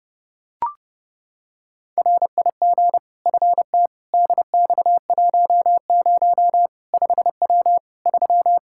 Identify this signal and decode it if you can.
Morse code, heard as a keyed tone, about 30 wpm, 700 Hz. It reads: RIG FT DX10 5W 3
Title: Practice QSO - Exchange 2 - Info - 30wpm